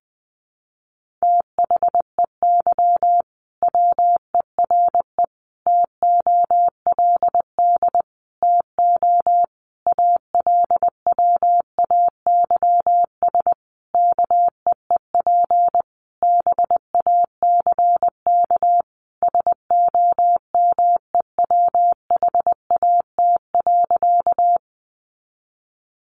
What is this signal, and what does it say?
Morse code, heard as a keyed tone, about 20 wpm, 700 Hz. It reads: THEY WERE TOLD TO ALWAYS KEEP BACK SOMEWHAT.
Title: Sentences from Top 1300 Words 20wpm